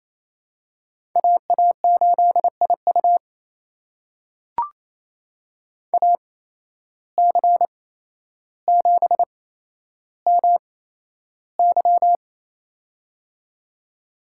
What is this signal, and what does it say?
Morse code, heard as a keyed tone, about 28 wpm, spaced as for 12 wpm, 700 Hz. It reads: AA8IU AC7MY
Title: USA Call Signs - 5 Characters - 8x Character Spacing - 28wpm